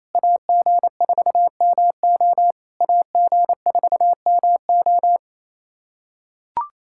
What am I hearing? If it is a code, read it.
Morse code, heard as a keyed tone, about 28 wpm, 700 Hz. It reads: AG4MO AG4MO
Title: SOTA QSO Practice 28wpm